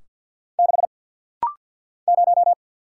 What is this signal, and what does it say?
Morse code, heard as a keyed tone, about 50 wpm, 700 Hz. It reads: 6 0
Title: Single Letter-Number - Rapid-Fire 50wpm